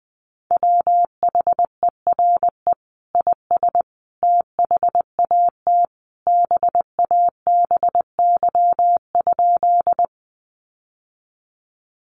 Morse code, heard as a keyed tone, about 20 wpm, 700 Hz. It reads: WHERE IS THAT BABY?
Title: Sentences from Top 1300 Words 20wpm